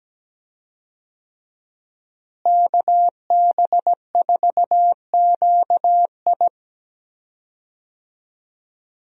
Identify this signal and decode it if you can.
Morse code, heard as a keyed tone, about 17 wpm, 700 Hz. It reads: KB4QI
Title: USA Call Signs - 5 Characters - Repeated 3x - 17wpm